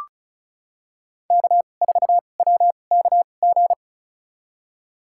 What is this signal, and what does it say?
Morse code, heard as a keyed tone, about 36 wpm, spaced as for 25 wpm, 700 Hz. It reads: K4WKG
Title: USA Call Signs - 2x Character Spacing - 35wpm